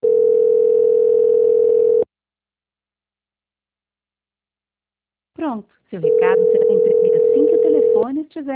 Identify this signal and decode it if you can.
Ringback tone (North America), 440 + 480 Hz, with speech.
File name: out- -2010- - - .63